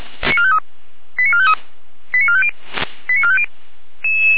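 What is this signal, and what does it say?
Five-tone selective call (ZVEI), heard as quick runs of five tones, tones of about 70 ms. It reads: 89453 89450 89450
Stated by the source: Received via SDR, ZVEI five-tone call.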